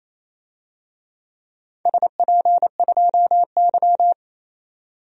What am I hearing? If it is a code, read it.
Morse code, heard as a keyed tone, about 28 wpm, 700 Hz. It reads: SP2Y
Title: DXCC-100 Call Signs - 4 Characters - 28wpm